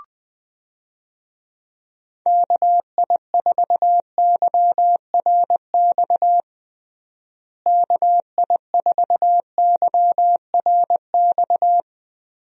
Morse code, heard as a keyed tone, about 20 wpm, 700 Hz. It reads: KI4YRX KI4YRX
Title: USA Call Signs - 6 Characters - Repeated 2x - 20wpm